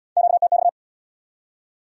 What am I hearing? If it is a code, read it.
Morse code, heard as a keyed tone, about 75 wpm, 700 Hz. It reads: QTX